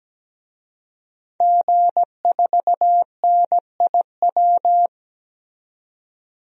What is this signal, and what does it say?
Morse code, heard as a keyed tone, about 17 wpm, 700 Hz. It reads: G4NIW
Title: DXCC-100 Call Signs - 5 Characters - 17wpm